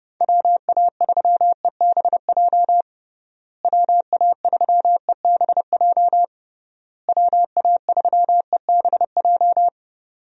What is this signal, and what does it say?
Morse code, heard as a keyed tone, about 30 wpm, 700 Hz. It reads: WA3EBJ WA3EBJ WA3EBJ
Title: USA Call Signs - 6 Characters - Repeated 3x - 30wpm